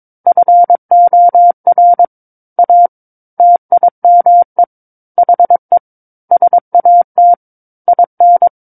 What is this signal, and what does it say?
Morse code, heard as a keyed tone, about 22 wpm, 700 Hz. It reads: FOR A TIME HE SAT IN
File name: Book One / 13-Chapter 13 - 22 WPM